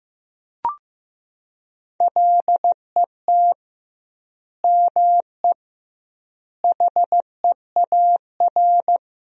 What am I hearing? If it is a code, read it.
Morse code, heard as a keyed tone, about 15 wpm, 700 Hz. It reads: LET ME HEAR
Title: 2x Word-Spacing - Sentences from Top 300 Words - 15wpm